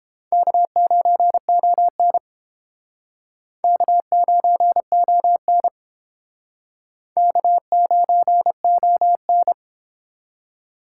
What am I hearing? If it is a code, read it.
Morse code, heard as a keyed tone, about 29 wpm, 700 Hz. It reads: K9ON K9ON K9ON